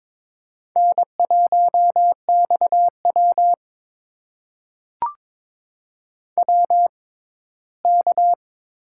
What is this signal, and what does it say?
Morse code, heard as a keyed tone, about 22 wpm, spaced as for 15 wpm, 700 Hz. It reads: N1XW WK
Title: USA Call Signs - 4 Characters - 6x Character Spacing - 22wpm